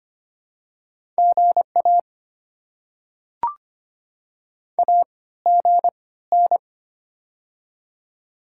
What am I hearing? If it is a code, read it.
Morse code, heard as a keyed tone, about 25 wpm, spaced as for 16 wpm, 690 Hz. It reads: GA AGN
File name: Top 90 QSO Elements - 3x Character Spacing - 25wpm